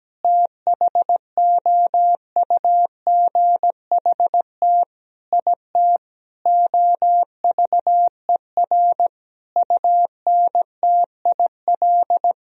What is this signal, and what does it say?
Morse code, heard as a keyed tone, about 17 wpm, 700 Hz. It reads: THOUGHT IT OVER UNTIL
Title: Sentences from Top 1000 Words - Review Part 1 - 17wpm